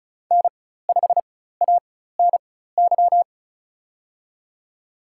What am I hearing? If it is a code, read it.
Morse code, heard as a keyed tone, about 34 wpm, spaced as for 16 wpm, 700 Hz. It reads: N5ANY
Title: USA Call Signs - 5 Characters - 4x Character Spacing - 35wpm